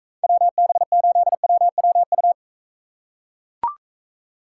Morse code, heard as a keyed tone, about 43 wpm, 700 Hz. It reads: WB8WWU